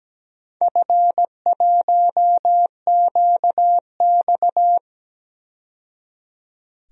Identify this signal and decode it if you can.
Morse code, heard as a keyed tone, about 17 wpm, 700 Hz. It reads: F1QX
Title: DXCC-100 Call Signs - 4 Characters - 17wpm